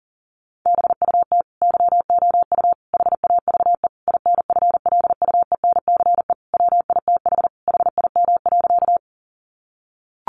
Morse code, heard as a keyed tone, about 40 wpm, 700 Hz. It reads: BUT YOU HAVE INFLUENCE WITH HIM.
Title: Sentences from Top 1100 Words 40wpm